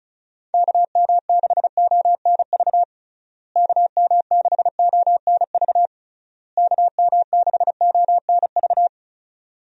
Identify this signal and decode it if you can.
Morse code, heard as a keyed tone, about 34 wpm, 700 Hz. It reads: KM6ONV KM6ONV KM6ONV
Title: USA Call Signs - 6 Characters - Repeated 3x - 35wpm